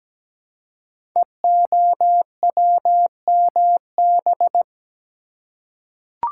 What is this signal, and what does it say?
Morse code, heard as a keyed tone, about 17 wpm, 700 Hz. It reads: EOWMB
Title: Code Groups - 5 Letter:Numbers - 17wpm